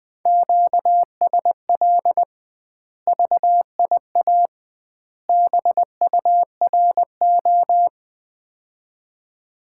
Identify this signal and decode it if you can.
Morse code, heard as a keyed tone, about 20 wpm, 700 Hz. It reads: QSL VIA BURO
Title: Common QSO Phrases - 2x Word-Spacing - 20wpm